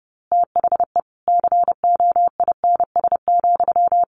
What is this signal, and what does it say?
Morse code, heard as a keyed tone, about 28 wpm, 700 Hz. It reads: THE COINS,
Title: A Sound Of Thunder 30wpm